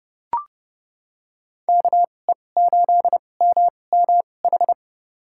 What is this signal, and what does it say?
Morse code, heard as a keyed tone, about 30 wpm, spaced as for 22 wpm, 700 Hz. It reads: KE8MMH
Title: USA Call Signs - 6 Characters - 2x Character Spacing - 30wpm